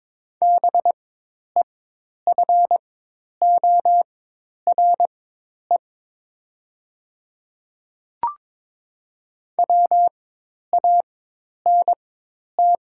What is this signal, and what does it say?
Morse code, heard as a keyed tone, about 22 wpm, spaced as for 10 wpm, 700 Hz. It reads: BEFORE WANT
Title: 4x Character Spacing - Top 300 Words - 22wpm